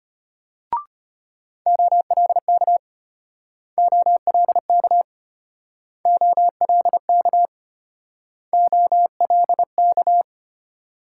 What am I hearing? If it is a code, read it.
Morse code, heard as a keyed tone, about 33 wpm, 700 Hz. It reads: OLK OLK OLK OLK